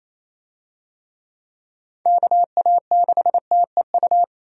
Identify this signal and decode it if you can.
Morse code, heard as a keyed tone, about 27 wpm, 700 Hz. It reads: KA6TEU
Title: USA Call Signs - 6 Characters - Repeated 2x - 28wpm